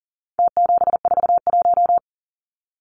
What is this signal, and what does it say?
Morse code, heard as a keyed tone, about 40 wpm, 700 Hz. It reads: T741
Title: Numbers - Contest Style - 4-Digit 40wpm